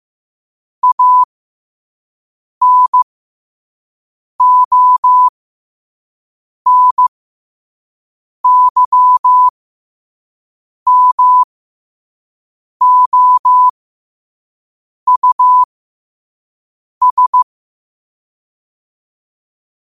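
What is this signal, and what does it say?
Morse code, heard as a keyed tone, about 15 wpm, spaced as for 5 wpm, 1000 Hz. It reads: ANONYMOUS
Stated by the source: Morse code quote of the day at 5 wpm.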